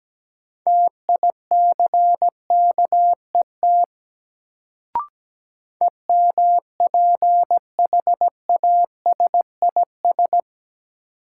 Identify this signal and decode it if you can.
Morse code, heard as a keyed tone, about 17 wpm, 700 Hz. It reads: TICKET EMPHASIS
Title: Sentences from Top 1400 Words - New Word Drill Alt - 17wpm